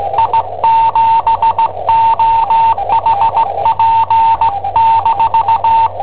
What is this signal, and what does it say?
Morse code, heard as a keyed tone, about 18 wpm, 900 Hz, with radio noise. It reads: I7OHP-
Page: Cassetta esercizi preparata da I7OHP per IW5ELP